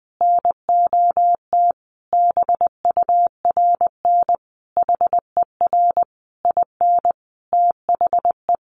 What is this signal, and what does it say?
Morse code, heard as a keyed tone, about 20 wpm, 700 Hz. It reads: NOT BURN HER IN THE
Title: Sentences from Top 1300 Words 20wpm